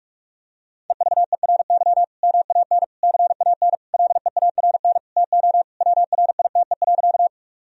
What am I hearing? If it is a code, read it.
Morse code, heard as a keyed tone, about 46 wpm, 700 Hz. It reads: EVERY MAN CAN LEARN TO WRITE.
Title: Sentences from Top 1000 Words - Review Part 1 - 45wpm